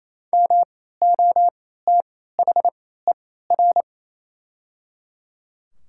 Morse code, heard as a keyed tone, about 28 wpm, spaced as for 16 wpm, 700 Hz. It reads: MOTHER
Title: 3x Character Spacing - Top 300 Words - 28wpm